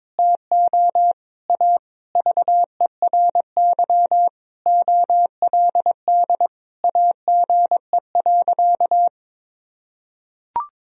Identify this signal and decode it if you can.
Morse code, heard as a keyed tone, about 22 wpm, 690 Hz. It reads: TO A VERY OLD AGE.
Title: Sentences from Top 1000 Words - Review Part 3 - 22wpm